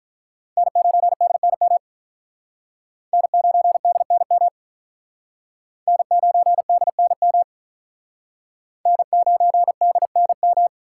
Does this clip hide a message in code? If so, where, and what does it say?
Yes: Morse code from 0.6 to 10.7 s, heard as a keyed tone, about 46 wpm, 700 Hz. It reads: N9DNM N9DNM N9DNM N9DNM